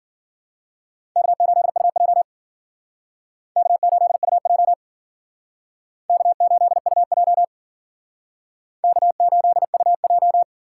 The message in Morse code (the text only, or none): K8UJ K8UJ K8UJ K8UJ